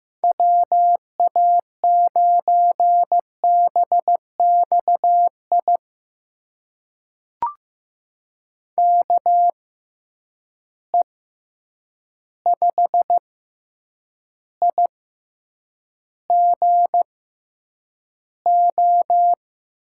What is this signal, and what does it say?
Morse code, heard as a keyed tone, about 15 wpm, spaced as for 8 wpm, 700 Hz. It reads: WA9BXI KE5IGO
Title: USA Call Signs - 6 Characters - 6x Character Spacing - 15wpm